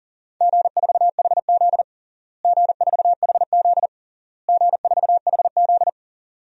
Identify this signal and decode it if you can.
Morse code, heard as a keyed tone, about 40 wpm, 700 Hz. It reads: G4HZ G4HZ G4HZ
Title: DXCC-100 Call Signs - 4 Characters - Repeated 3x - 40wpm